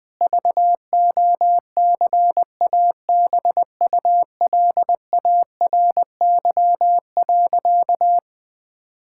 Morse code, heard as a keyed tone, about 20 wpm, 700 Hz. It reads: VOCABULARY.
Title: Difficult Sentences with Hints - 20wpm